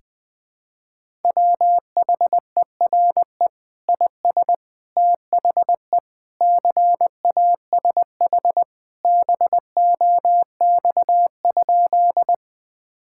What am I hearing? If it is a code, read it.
Morse code, heard as a keyed tone, about 20 wpm, 700 Hz. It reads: WHERE IS THE CASH BOX?
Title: Sentences from Top 1000 Words - Review Part 5 - 20wpm